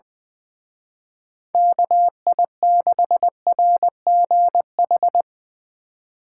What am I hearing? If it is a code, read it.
Morse code, heard as a keyed tone, about 22 wpm, 700 Hz. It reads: KI6RGH
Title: Speed Racing - USA 2x3 Call Signs 20wpm